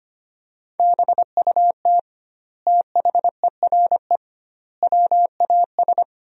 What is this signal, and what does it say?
Morse code, heard as a keyed tone, about 25 wpm, 700 Hz. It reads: BUT THERE WAS
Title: 2x Word-Spacing - Sentences from Top 100 Words - 25wpm